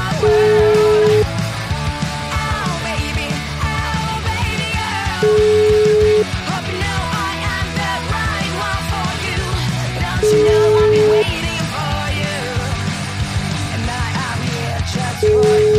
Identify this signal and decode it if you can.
Ringback tone (Europe, 425 Hz), with noise.